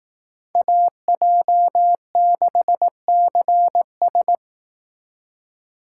Morse code, heard as a keyed tone, about 18 wpm, 700 Hz. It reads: AJ6CS
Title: Speed Racing - USA 2x2 Call Signs 15wpm